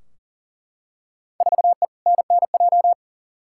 Morse code, heard as a keyed tone, about 40 wpm, 700 Hz. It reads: 4E NNJ
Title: Field Day Exchanges - 40wpm